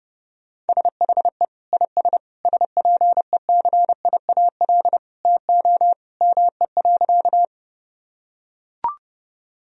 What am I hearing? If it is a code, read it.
Morse code, heard as a keyed tone, about 30 wpm, 700 Hz. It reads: SHE IS SPECIAL TO ME.
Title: Sentences from Top 1000 Words - Review Part 3 - 30wpm